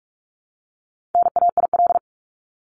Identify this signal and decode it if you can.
Morse code, heard as a keyed tone, about 44 wpm, 700 Hz. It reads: NAIL
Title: Top 1501-1750 Words 45wpm